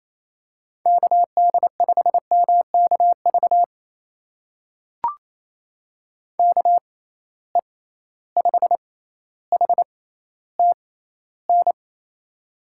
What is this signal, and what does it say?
Morse code, heard as a keyed tone, about 28 wpm, spaced as for 14 wpm, 700 Hz. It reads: KD5MKV KE5HTN
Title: USA Call Signs - 6 Characters - 6x Character Spacing - 28wpm